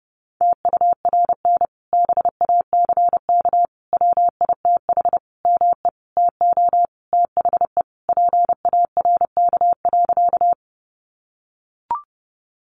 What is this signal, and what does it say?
Morse code, heard as a keyed tone, about 30 wpm, 700 Hz. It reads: TURN BACK WITH ME TO THE PARK.
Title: Sentences from Top 1300 Words 30wpm